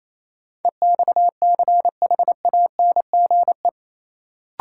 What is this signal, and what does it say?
Morse code, heard as a keyed tone, about 28 wpm, 700 Hz. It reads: EXCHANGE
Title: Sentences from Top 1000 Words - Review Part 6 - New Word Drill - 28wpm